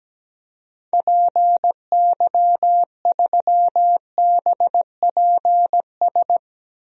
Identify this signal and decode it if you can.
Morse code, heard as a keyed tone, about 17 wpm, 700 Hz. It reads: PY3BPS
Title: DXCC-100 Call Signs - 6 Characters - Repeated 2x - 17wpm